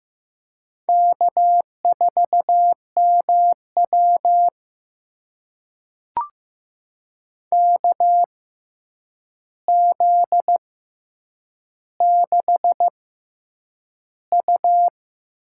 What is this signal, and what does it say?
Morse code, heard as a keyed tone, about 15 wpm, spaced as for 8 wpm, 700 Hz. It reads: K4MW KZ6U
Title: USA Call Signs - 4 Characters - 6x Character Spacing - 15wpm